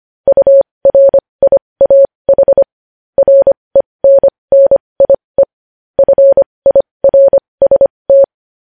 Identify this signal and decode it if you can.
Morse code, heard as a keyed tone, about 25 wpm, spaced as for 20 wpm, 550 Hz. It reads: URIAH RENNIE FIRST
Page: News Headlines in Morse Code at 20 WPM